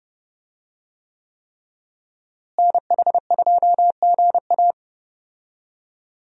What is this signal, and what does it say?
Morse code, heard as a keyed tone, about 32 wpm, 700 Hz. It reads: NH2GA
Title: USA Call Signs - 5 Characters - Repeated 3x - 30wpm